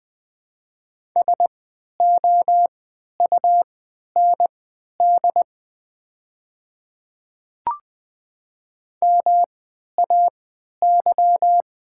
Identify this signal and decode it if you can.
Morse code, heard as a keyed tone, about 21 wpm, spaced as for 11 wpm, 700 Hz. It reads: SOUND MAY
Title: 3x Character Spacing - Top 100 Words - 20wpm